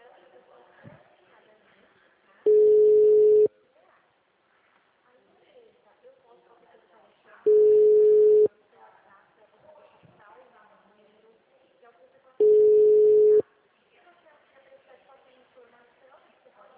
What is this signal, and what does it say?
Ringback tone (Europe, 425 Hz), with speech.